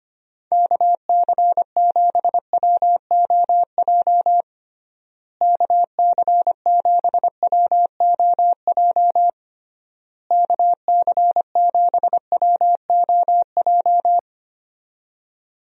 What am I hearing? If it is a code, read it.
Morse code, heard as a keyed tone, about 25 wpm, 700 Hz. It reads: KC7WOJ KC7WOJ KC7WOJ
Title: USA Call Signs - 6 Characters - Repeated 3x - 25wpm